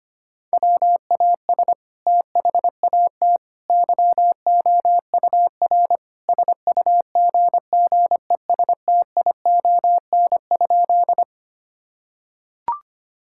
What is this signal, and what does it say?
Morse code, heard as a keyed tone, about 25 wpm, 700 Hz. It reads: WAS THAT YOUR SUGGESTION?
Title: Sentences from Top 1600 Words 25wpm